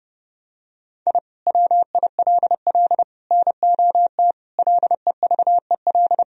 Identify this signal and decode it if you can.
Morse code, heard as a keyed tone, about 31 wpm, 700 Hz. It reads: I WILL NOT LEVEL
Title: Sentences from Top 1000 Words - Review Part 4 - 30wpm